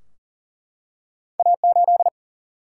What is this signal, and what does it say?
Morse code, heard as a keyed tone, about 40 wpm, 700 Hz. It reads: A 8
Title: Single-Letter Single-Number 40wpm